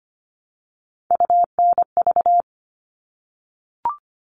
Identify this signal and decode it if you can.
Morse code, heard as a keyed tone, about 25 wpm, 700 Hz. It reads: UNV
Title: All Cut Numbers - 3-Digit 25wpm